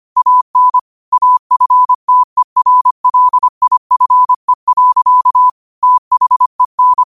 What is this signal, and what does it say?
Morse code, heard as a keyed tone, about 25 wpm, 1000 Hz. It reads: AN AFTERLIFE. THEN